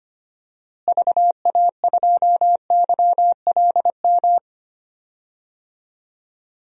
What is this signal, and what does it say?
Morse code, heard as a keyed tone, about 25 wpm, 700 Hz. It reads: VA2YLM
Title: DXCC-100 Call Signs - 6 Characters - Repeated 2x - 25wpm